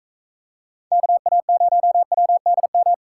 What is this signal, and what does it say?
Morse code, heard as a keyed tone, about 40 wpm, 700 Hz. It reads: KA0WDM